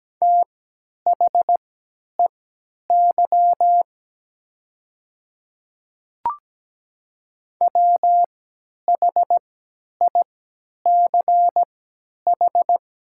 Morse code, heard as a keyed tone, about 17 wpm, spaced as for 10 wpm, 690 Hz. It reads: THEY WHICH
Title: 3x Character Spacing - Top 100 Words - 17wpm